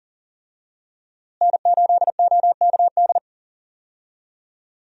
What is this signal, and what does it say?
Morse code, heard as a keyed tone, about 40 wpm, 700 Hz. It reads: N8OKD
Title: USA Call Signs - 5 Characters - Repeated 2x - 40wpm